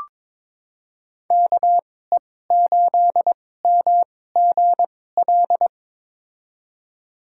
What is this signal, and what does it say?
Morse code, heard as a keyed tone, about 22 wpm, spaced as for 16 wpm, 700 Hz. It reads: KE8MGL
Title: USA Call Signs - 6 Characters - 2x Character Spacing - 22wpm